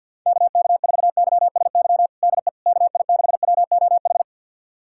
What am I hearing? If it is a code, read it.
Morse code, heard as a keyed tone, about 52 wpm, 700 Hz. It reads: KK4YSY DE KI6WOH
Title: USA Callsign DE Callsign 50wpm